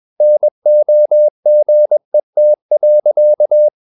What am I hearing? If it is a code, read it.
Morse code, heard as a keyed tone, about 21 wpm, 600 Hz. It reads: NOGET.
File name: Ferie QRS 21wpm